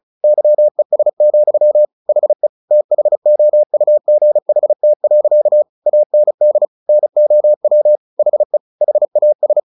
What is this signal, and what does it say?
Morse code, heard as a keyed tone, about 34 wpm, 600 Hz. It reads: YES, HE THOUGHT. AND NOW HE HAS
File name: the old man and the sea 2 35wpm